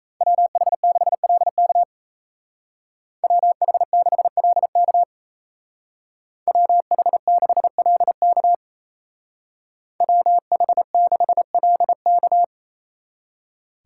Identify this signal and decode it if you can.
Morse code, heard as a keyed tone, about 33 wpm, 700 Hz. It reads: WH6LK WH6LK WH6LK WH6LK
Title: Speed Racing - USA 2x2 Call Signs 28wpm